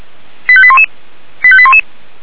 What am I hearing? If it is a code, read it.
Five-tone selective call (ZVEI), heard as quick runs of five tones, tones of about 70 ms. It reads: 85811 85811